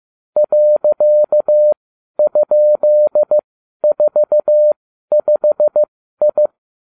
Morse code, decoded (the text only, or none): .?45I